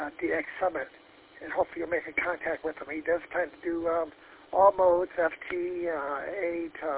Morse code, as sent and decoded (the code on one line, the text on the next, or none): none